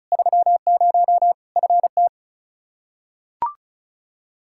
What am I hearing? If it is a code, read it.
Morse code, heard as a keyed tone, about 34 wpm, 700 Hz. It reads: 30 FT
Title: Practice QSO - Exchange 2 - Info - 35wpm